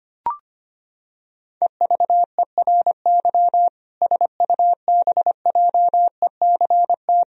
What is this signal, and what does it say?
Morse code, heard as a keyed tone, about 25 wpm, 700 Hz. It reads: EVERY SUBJECT
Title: Sentences from Top 1000 Words - Review Part 1 - 25wpm